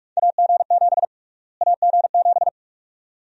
Morse code, heard as a keyed tone, about 46 wpm, 700 Hz. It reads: AG7 AG7
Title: USA Call Sign Prefixes - Repeated 2x - 45wpm